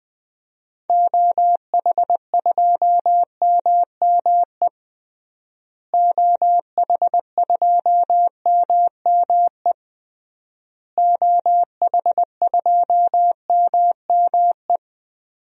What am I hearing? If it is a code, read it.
Morse code, heard as a keyed tone, about 20 wpm, 700 Hz. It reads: OH2MME OH2MME OH2MME
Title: DXCC-100 Call Signs - 6 Characters - Repeated 3x - 20wpm